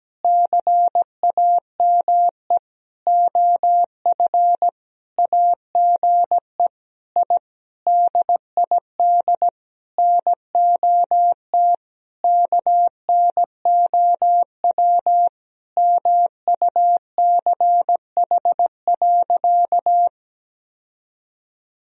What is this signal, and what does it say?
Morse code, heard as a keyed tone, about 17 wpm, 700 Hz. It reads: CAME OF AGE I DID NOT KNOW MUCH.
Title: Sentences from Top 1000 Words - Review Part 3 - 17wpm